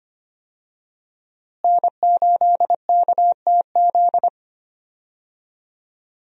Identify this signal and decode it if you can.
Morse code, heard as a keyed tone, about 25 wpm, 700 Hz. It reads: N8KTZ
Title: USA Call Signs - 5 Characters - Repeated 2x - 25wpm